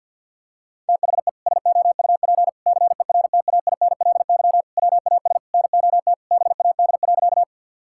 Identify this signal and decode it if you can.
Morse code, heard as a keyed tone, about 50 wpm, 700 Hz. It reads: THE SOUP CERTAINLY WAS NOT BAD.
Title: Sentences from Top 1000 Words - Review Part 6 - 50wpm